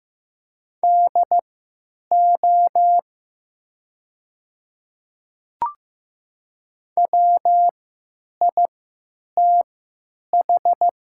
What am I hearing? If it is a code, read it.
Morse code, heard as a keyed tone, about 15 wpm, spaced as for 9 wpm, 700 Hz. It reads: DO WITH
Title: 3x Character Spacing - Top 300 Words - 15wpm